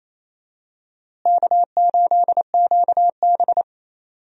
Morse code, heard as a keyed tone, about 28 wpm, 700 Hz. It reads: K8QB
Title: USA Call Signs - 4 Characters - Repeated 3x - 28wpm